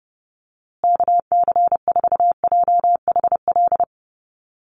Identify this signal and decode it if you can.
Morse code, heard as a keyed tone, about 30 wpm, 700 Hz. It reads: KC4JHL
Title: USA Call Signs Repeated 2x 30wpm